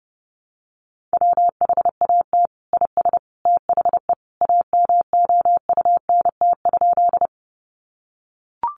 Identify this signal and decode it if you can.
Morse code, heard as a keyed tone, about 30 wpm, 700 Hz. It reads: WHAT IS THE AMOUNT?
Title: Sentences from Top 700 Words 30wpm